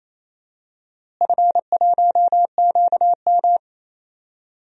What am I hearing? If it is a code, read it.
Morse code, heard as a keyed tone, about 27 wpm, 700 Hz. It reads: F1QM
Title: DXCC-100 Call Signs - 4 Characters - Repeated 3x - 28wpm